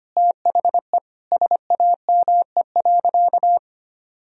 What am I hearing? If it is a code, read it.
Morse code, heard as a keyed tone, about 25 wpm, 700 Hz. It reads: THE SAME.
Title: Sentences from Top 1000 Words - Review Part 1 - 25wpm